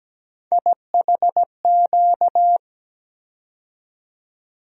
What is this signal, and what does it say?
Morse code, heard as a keyed tone, about 17 wpm, 700 Hz. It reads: IHQ
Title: USA Call Sign Postfixes - 3 Letters 17wpm